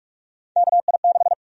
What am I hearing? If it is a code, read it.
Morse code, heard as a keyed tone, about 46 wpm, 700 Hz. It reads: KI6
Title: USA Call Sign Prefixes - Repeated 2x - 45wpm